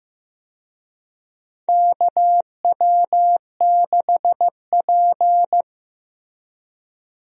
Morse code, heard as a keyed tone, about 15 wpm, 700 Hz. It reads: KW6P
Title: USA Call Signs - 4 Characters - Repeated 2x - 15wpm